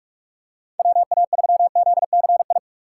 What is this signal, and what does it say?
Morse code, heard as a keyed tone, about 44 wpm, 700 Hz. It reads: WA3ZCI
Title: USA Call Signs - 6 Characters - 6x Character Spacing - 45wpm